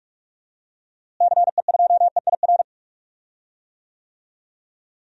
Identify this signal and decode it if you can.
Morse code, heard as a keyed tone, about 42 wpm, 700 Hz. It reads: CE2EIR
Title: DXCC-100 Call Signs - 6 Characters - Repeated 3x - 45wpm